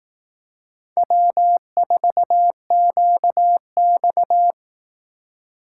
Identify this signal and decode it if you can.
Morse code, heard as a keyed tone, about 18 wpm, 700 Hz. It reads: W4QX
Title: Speed Racing - USA 1x2 Call Signs 15wpm